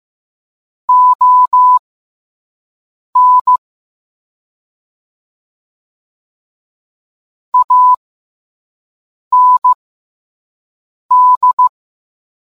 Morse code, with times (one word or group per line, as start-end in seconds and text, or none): 0.9-3.6 ON
7.5-11.7 AND